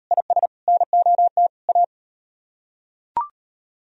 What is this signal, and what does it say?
Morse code, heard as a keyed tone, about 36 wpm, 700 Hz. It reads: IS NOT A